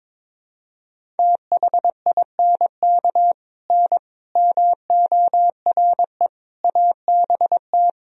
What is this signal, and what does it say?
Morse code, heard as a keyed tone, about 22 wpm, 700 Hz. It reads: THINK N MORE ABT
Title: Sentences from Top 200 Words and Abbreviations 22wpm